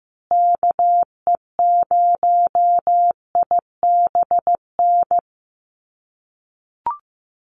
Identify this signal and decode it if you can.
Morse code, heard as a keyed tone, about 15 wpm, 700 Hz. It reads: KE0IBN
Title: USA Call Signs Repeated 3x 15wpm